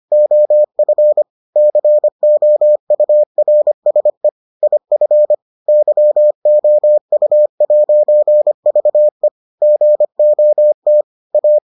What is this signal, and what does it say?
Morse code, heard as a keyed tone, about 25 wpm, 600 Hz. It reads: OF COURSE IF YOU'VE GOT A